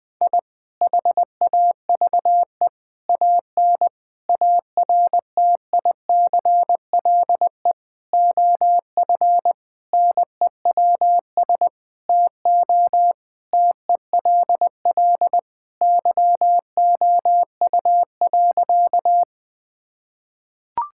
Morse code, heard as a keyed tone, about 20 wpm, 700 Hz. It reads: I HAVE AN ARTICLE OF NEWS TO TELL YOU.
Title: Sentences from Top 1000 Words - Review Part 5 - 20wpm